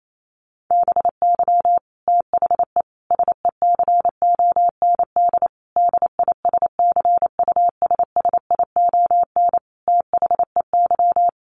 Morse code, heard as a keyed tone, about 28 wpm, 700 Hz. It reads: BY THE SECOND DISCUSSION THEY
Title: Sentences from Top 1200 Words 28wpm